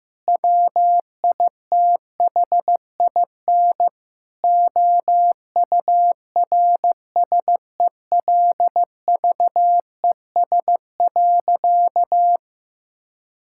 Morse code, heard as a keyed tone, about 16 wpm, 700 Hz. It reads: WITHIN OURSELVES.
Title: Sentences from Top 1600 Words 15wpm